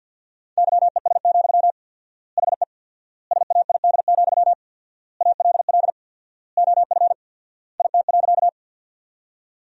Morse code, heard as a keyed tone, about 53 wpm, 700 Hz. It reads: YES, HE SAID, ALL OF IT.
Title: 4x Word-Spacing - Sentences from Top 500 Words - 50wpm